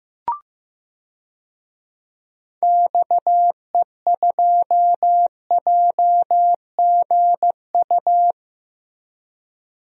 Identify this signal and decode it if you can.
Morse code, heard as a keyed tone, about 15 wpm, 700 Hz. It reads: XE2JGU
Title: DXCC-100 Call Signs - 6 Characters - Repeated 2x - 15wpm